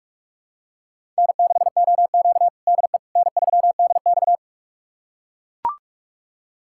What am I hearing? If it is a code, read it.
Morse code, heard as a keyed tone, about 42 wpm, 700 Hz. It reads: N6OQ DE N3DX
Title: USA Callsign DE Callsign - 4 Characters - 45wpm